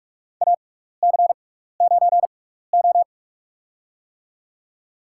Morse code, heard as a keyed tone, about 42 wpm, spaced as for 15 wpm, 700 Hz. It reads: AC9O
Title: USA Call Signs - 4 Characters - 6x Character Spacing - 45wpm